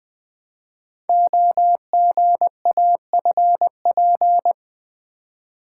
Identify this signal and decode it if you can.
Morse code, heard as a keyed tone, about 20 wpm, 700 Hz. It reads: OGAFP
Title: Code Groups - 5 Letters - 20wpm